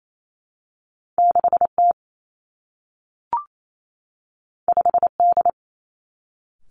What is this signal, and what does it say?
Morse code, heard as a keyed tone, about 28 wpm, 700 Hz. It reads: 6 T 5 D
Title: Two Challenging Letter-Number Combinations 28wpm